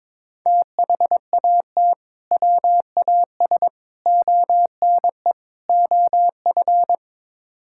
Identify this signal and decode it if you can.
Morse code, heard as a keyed tone, about 22 wpm, 700 Hz. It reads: THAT WAS ONE OF
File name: practice / 22wpm / Build A Phrase - 5-Word Phrases - 22wpm